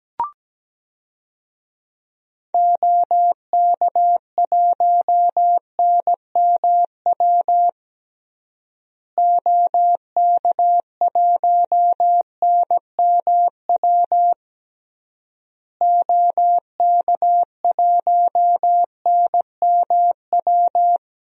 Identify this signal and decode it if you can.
Morse code, heard as a keyed tone, about 17 wpm, 700 Hz. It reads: OK1NMW OK1NMW OK1NMW
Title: DXCC-100 Call Signs - 6 Characters - Repeated 3x - 17wpm